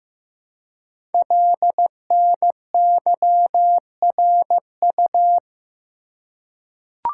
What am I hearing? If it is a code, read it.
Morse code, heard as a keyed tone, about 15 wpm, 700 Hz. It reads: LNYRU
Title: Code Groups - 5 Letter:Numbers - 15wpm